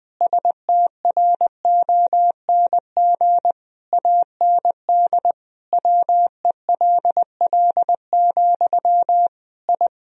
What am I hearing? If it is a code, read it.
Morse code, heard as a keyed tone, about 20 wpm, 700 Hz. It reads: STRONG AND WELL, I
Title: Sentences from Top 1000 Words - Review Part 3 - 20wpm